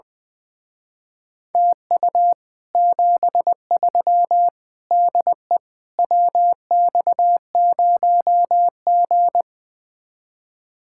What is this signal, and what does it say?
Morse code, heard as a keyed tone, about 20 wpm, 700 Hz. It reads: TU 73 DE WX0G
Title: Speed-Racing Common QSO Phrases - 20wpm